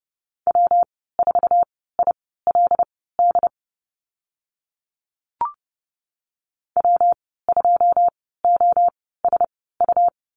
Morse code, heard as a keyed tone, about 30 wpm, spaced as for 17 wpm, 700 Hz. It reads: W4ILD W2OSU
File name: Lesson 19g - Letter w - Call Signs 30wpm